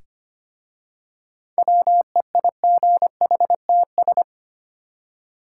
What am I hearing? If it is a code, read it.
Morse code, heard as a keyed tone, about 25 wpm, 700 Hz. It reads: WEIGHTS
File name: practice / 25wpm / Easy Words with Hints - 25wpm